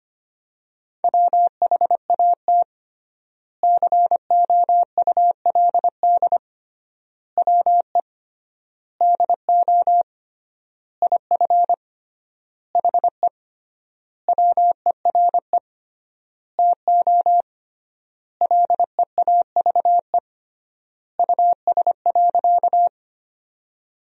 Morse code, heard as a keyed tone, about 25 wpm, 700 Hz. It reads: WHAT COULD WE DO IF HE WERE TO LEAVE US.
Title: 3x Word-Spacing - Sentences from Top 400 Words - 25wpm